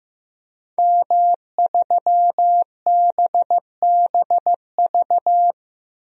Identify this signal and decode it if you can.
Morse code, heard as a keyed tone, about 15 wpm, 700 Hz. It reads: M3BBV
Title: DXCC-100 Call Signs - 5 Characters - 15wpm